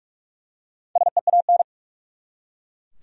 Morse code, heard as a keyed tone, about 48 wpm, 700 Hz. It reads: SEAN
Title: USA Common Names - Max 5 Letters - 45wpm